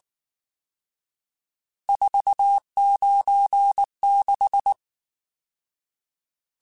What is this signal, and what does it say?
Morse code, heard as a keyed tone, about 19 wpm, 800 Hz. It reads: 496
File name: RSTs / RST-mp3s / Various RST @18